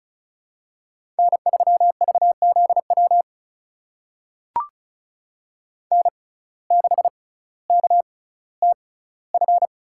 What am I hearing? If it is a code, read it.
Morse code, heard as a keyed tone, about 36 wpm, spaced as for 18 wpm, 700 Hz. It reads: N3VZW N6KTF
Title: USA Call Signs - 5 Characters - 6x Character Spacing - 35wpm